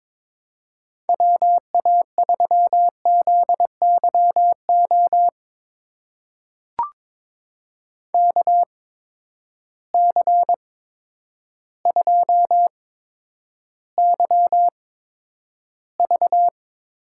Morse code, heard as a keyed tone, about 22 wpm, spaced as for 10 wpm, 700 Hz. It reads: WA3ZYO KC2YV
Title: USA Call Signs - 6 Characters - 8x Character Spacing - 22wpm